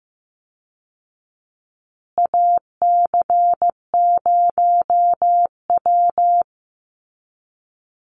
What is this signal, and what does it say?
Morse code, heard as a keyed tone, about 15 wpm, 700 Hz. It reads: AC0W
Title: Call Signs Repeated 2x - 15wpm